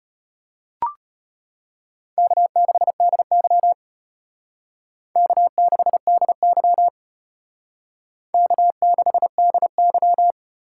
Morse code, heard as a keyed tone, about 32 wpm, 700 Hz. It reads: K6DY K6DY K6DY